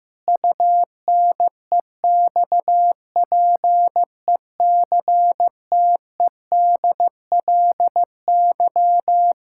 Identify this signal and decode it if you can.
Morse code, heard as a keyed tone, about 15 wpm, 700 Hz. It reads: UNEXPECTEDLY
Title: Pangram Challenge v2 - 15wpm